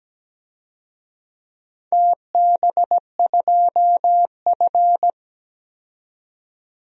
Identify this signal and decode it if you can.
Morse code, heard as a keyed tone, about 17 wpm, 700 Hz. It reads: TB2F
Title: DXCC-100 Call Signs - 4 Characters - 17wpm